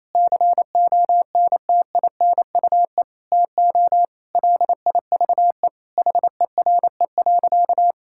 Morse code, heard as a keyed tone, about 27 wpm, 700 Hz. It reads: CONTINUE TO LIVE HERE.
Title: Sentences from Top 1000 Words - Review Part 6 - 28wpm